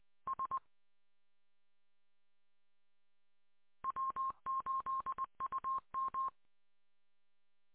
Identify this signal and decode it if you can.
Morse code, heard as a keyed tone, about 20 wpm, 1060 Hz. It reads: S W8UM